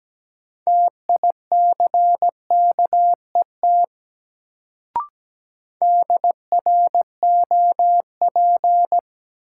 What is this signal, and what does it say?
Morse code, heard as a keyed tone, about 17 wpm, 700 Hz. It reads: TICKET DROP
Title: Sentences from Top 1400 Words - New Word Drill Alt - 17wpm